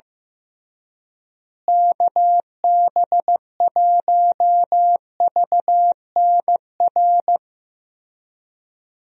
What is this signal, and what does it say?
Morse code, heard as a keyed tone, about 15 wpm, 700 Hz. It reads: KB1VNR
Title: USA Call Signs - 6 Characters - Repeated 3x - 15wpm